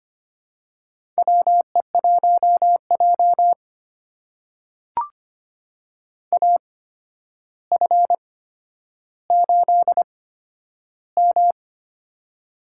Morse code, heard as a keyed tone, about 25 wpm, spaced as for 11 wpm, 700 Hz. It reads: WE1J AF8M
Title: USA Call Signs - 4 Characters - 8x Character Spacing - 25wpm